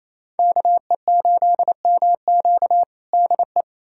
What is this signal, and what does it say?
Morse code, heard as a keyed tone, about 28 wpm, 700 Hz. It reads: KE8MQ DE
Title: USA Callsign DE Callsign 28wpm